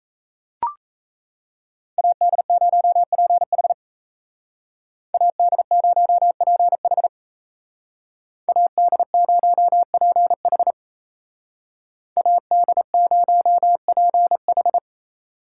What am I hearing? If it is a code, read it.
Morse code, heard as a keyed tone, about 36 wpm, 700 Hz. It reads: AD0PH AD0PH AD0PH AD0PH